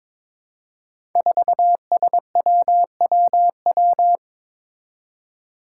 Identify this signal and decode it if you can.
Morse code, heard as a keyed tone, about 22 wpm, 700 Hz. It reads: 4SWWW
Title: Five Challenging Letter-Number Combinations 22wpm